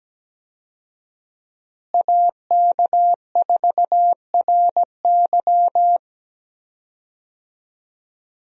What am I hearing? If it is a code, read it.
Morse code, heard as a keyed tone, about 17 wpm, 700 Hz. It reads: AK4RY
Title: USA Call Signs - 5 Characters - Repeated 3x - 17wpm